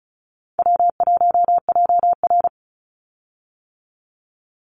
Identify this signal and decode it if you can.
Morse code, heard as a keyed tone, about 36 wpm, 700 Hz. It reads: W1JR
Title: Call Signs - 35wpm